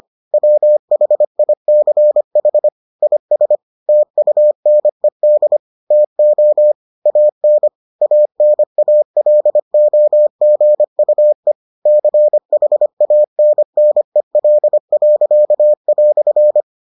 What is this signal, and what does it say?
Morse code, heard as a keyed tone, about 25 wpm, 600 Hz. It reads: WHICH IS TUNED TO AN ANALOGUE CHANNEL."